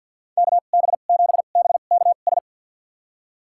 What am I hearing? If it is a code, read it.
Morse code, heard as a keyed tone, about 46 wpm, spaced as for 35 wpm, 700 Hz. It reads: KB7BKS
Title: USA Call Signs - 2x Character Spacing - 50wpm